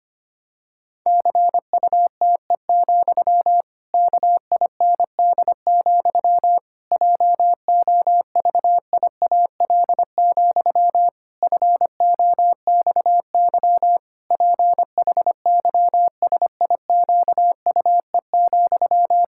Morse code, heard as a keyed tone, about 25 wpm, 700 Hz. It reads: CUTE, KIND, JOVIAL, FOXY PHYSIQUE,
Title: Pangram Challenge v2 - 25wpm